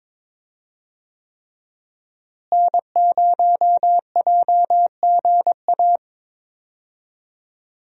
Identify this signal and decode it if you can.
Morse code, heard as a keyed tone, about 22 wpm, 700 Hz. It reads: N0JGA
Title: USA Call Signs - 5 Characters - Repeated 3x - 22wpm